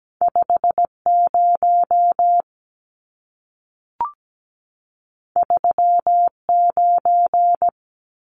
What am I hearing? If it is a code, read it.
Morse code, heard as a keyed tone, about 17 wpm, 700 Hz. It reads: 50 39
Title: Numbers - 2-Digit 17wpm